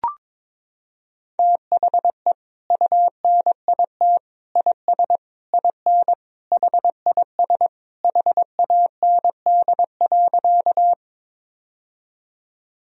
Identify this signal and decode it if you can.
Morse code, heard as a keyed tone, about 22 wpm, 700 Hz. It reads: THE UNIT IS IN HIS HAND.
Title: Sentences from Top 1000 Words - Review Part 2 - 22wpm